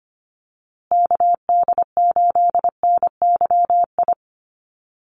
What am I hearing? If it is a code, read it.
Morse code, heard as a keyed tone, about 25 wpm, 700 Hz. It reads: KD8NYI
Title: USA Call Signs Repeated 3x 25wpm